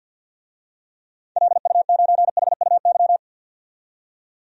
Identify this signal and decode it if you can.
Morse code, heard as a keyed tone, about 51 wpm, 700 Hz. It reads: LU9HUY
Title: DXCC-100 Call Signs - 6 Characters - 50wpm